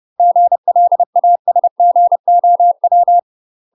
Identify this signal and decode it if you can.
Morse code, heard as a keyed tone, about 30 wpm, 700 Hz. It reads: GLASGOW
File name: UK MASTER QSOs TRAINING / 30WPM / UK MASTER QSOs 0034